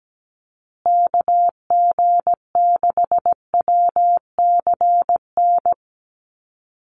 Morse code, heard as a keyed tone, about 17 wpm, 700 Hz. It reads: KG6WCN
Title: USA Call Signs Repeated 3x 17wpm